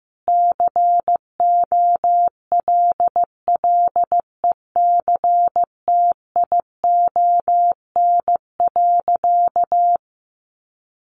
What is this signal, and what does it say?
Morse code, heard as a keyed tone, about 15 wpm, 700 Hz. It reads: COLLECTION.
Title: Sentences from Top 1200 Words 15wpm